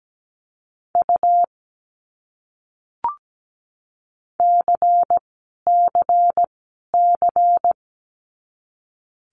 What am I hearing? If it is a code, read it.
Morse code, heard as a keyed tone, about 17 wpm, 700 Hz. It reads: U C C C
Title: Abbreviations Repeated 3x - 17wpm